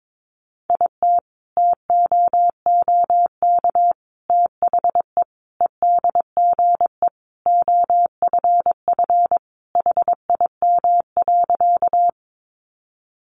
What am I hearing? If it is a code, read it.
Morse code, heard as a keyed tone, about 22 wpm, 700 Hz. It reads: IT TOOK THE EDGE OFF HIM.
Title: Sentences from Top 1300 Words 22wpm